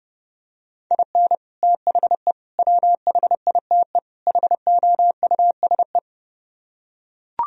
Morse code, heard as a keyed tone, about 31 wpm, 700 Hz. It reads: IN THE WHITE HOUSE